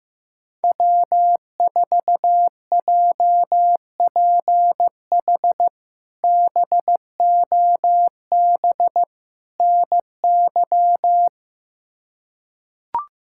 Morse code, heard as a keyed tone, about 15 wpm, 700 Hz. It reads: W4JPH BOB NY
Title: K1USN SST QSO Practice - 15wpm